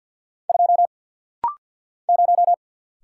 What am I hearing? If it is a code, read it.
Morse code, heard as a keyed tone, about 50 wpm, 700 Hz. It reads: 2 0
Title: Single Letter-Number - Rapid-Fire 50wpm